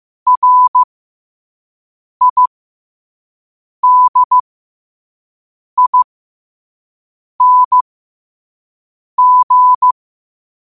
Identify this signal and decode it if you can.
Morse code, heard as a keyed tone, about 15 wpm, spaced as for 5 wpm, 1000 Hz. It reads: RIDING
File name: qotdfiles / 5wpm